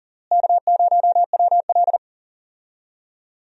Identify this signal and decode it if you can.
Morse code, heard as a keyed tone, about 37 wpm, 700 Hz. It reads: K0WL
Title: USA Call Signs - 4 Characters - Repeated 3x - 40wpm